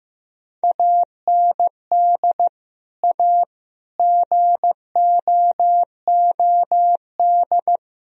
Morse code, heard as a keyed tone, about 15 wpm, 700 Hz. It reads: AND A GOOD
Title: Sentences from Top 1000 Words - Review Part 3 - 15wpm